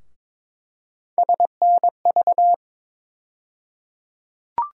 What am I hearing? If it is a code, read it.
Morse code, heard as a keyed tone, about 22 wpm, 700 Hz. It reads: SNV
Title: USA Call Sign Postfixes - 3 Letters 22wpm